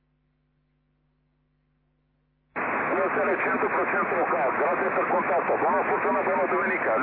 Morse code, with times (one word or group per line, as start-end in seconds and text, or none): none